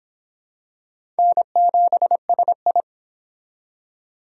Morse code, heard as a keyed tone, about 27 wpm, 700 Hz. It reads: N7SI
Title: Speed Racing - USA 1x2 Call Signs 22wpm